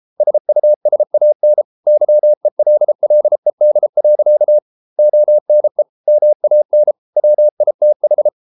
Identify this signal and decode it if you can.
Morse code, heard as a keyed tone, about 33 wpm, 600 Hz. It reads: SUSAN YELLED. ONE MAN WITH